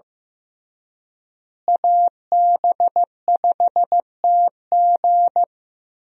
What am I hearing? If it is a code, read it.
Morse code, heard as a keyed tone, about 15 wpm, 700 Hz. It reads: AB5TG
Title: USA Call Signs - 5 Characters - Repeated 3x - 15wpm